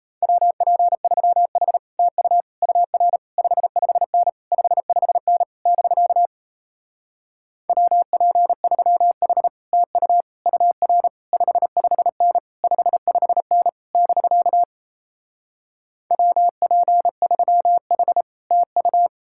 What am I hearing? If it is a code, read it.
Morse code, heard as a keyed tone, about 37 wpm, 700 Hz. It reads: WP3H TU UR 55N 55N <BK> WP3H TU UR 55N 55N <BK> WP3H TU